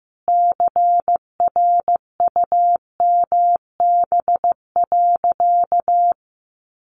Morse code, heard as a keyed tone, about 15 wpm, 700 Hz. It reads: CRUMB.
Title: Additional Easy Phrases 15wpm